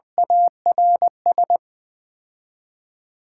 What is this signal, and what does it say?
Morse code, heard as a keyed tone, about 20 wpm, 700 Hz. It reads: ARS
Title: 3x Word-Spacing - Sets of 5 Words - 20wpm